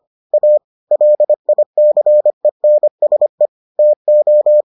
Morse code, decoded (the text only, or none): A LICENSE TO